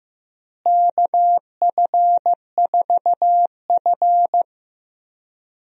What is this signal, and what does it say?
Morse code, heard as a keyed tone, about 15 wpm, 700 Hz. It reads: KF4F
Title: USA Call Signs - 4 Characters - Repeated 3x - 15wpm